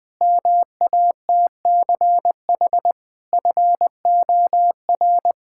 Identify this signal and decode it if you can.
Morse code, heard as a keyed tone, about 20 wpm, 700 Hz. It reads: MATCH FOR
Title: Sentences from Top 1600 Words 20wpm